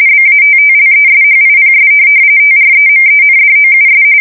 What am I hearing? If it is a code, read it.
RTTY, about 45.45 baud. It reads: BATTLESHIPS TREMENDOUS
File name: fsk170 / USS-New-Jersey